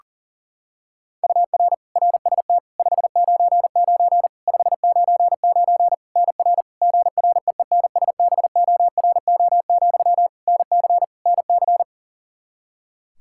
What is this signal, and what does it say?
Morse code, heard as a keyed tone, about 40 wpm, 700 Hz. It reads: UR RST 599 599 NR GREENSBORO, NC NC
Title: Practice QSO - Exchange 1 - Beginning - 40wpm